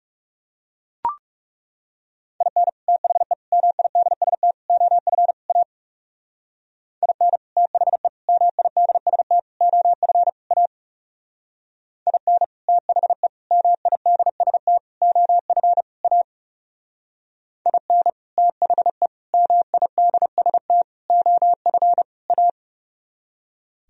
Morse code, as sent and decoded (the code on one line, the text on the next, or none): .. -. / - .... / -- .. -.. ... - / --- ..-. / .- / .. -. / - .... / -- .. -.. ... - / --- ..-. / .- / .. -. / - .... / -- .. -.. ... - / --- ..-. / .- / .. -. / - .... / -- .. -.. ... - / --- ..-. / .-
IN THE MIDST OF A IN THE MIDST OF A IN THE MIDST OF A IN THE MIDST OF A